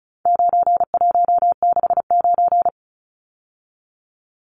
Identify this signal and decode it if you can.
Morse code, heard as a keyed tone, about 34 wpm, 700 Hz. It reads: 9169
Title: Numbers - 4-Digit 35wpm